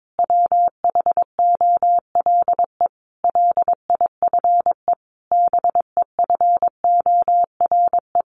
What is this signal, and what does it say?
Morse code, heard as a keyed tone, about 22 wpm, 700 Hz. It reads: WHOLE LIFE BEFORE
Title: Sentences from Top 400 Words 22wpm